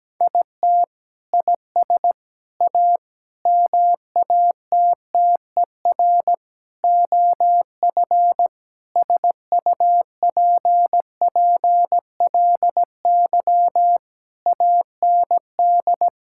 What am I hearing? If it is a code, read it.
Morse code, heard as a keyed tone, about 17 wpm, 700 Hz. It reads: IT IS A MATTER OF SUPPLY AND
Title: Sentences from Top 1000 Words - Review Part 5 - 17wpm